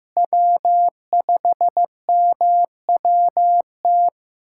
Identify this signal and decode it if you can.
Morse code, heard as a keyed tone, about 15 wpm, 700 Hz. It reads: W5MWT
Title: USA Callsign DE Callsign - 5 Characters - 15wpm